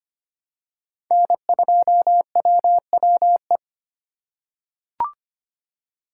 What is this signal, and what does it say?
Morse code, heard as a keyed tone, about 25 wpm, 700 Hz. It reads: N2WWE
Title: USA Call Signs - 5 Characters - 3x Character Spacing - 25wpm